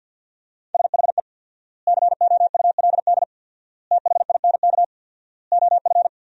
Morse code, heard as a keyed tone, about 50 wpm, 700 Hz. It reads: SHE COULD THINK OF
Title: 4x Word-Spacing - Sentences from Top 500 Words - 50wpm